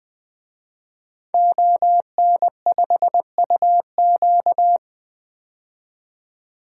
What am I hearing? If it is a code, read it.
Morse code, heard as a keyed tone, about 20 wpm, 700 Hz. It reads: ON5UQ
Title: DXCC-100 Call Signs - 5 Characters - 20wpm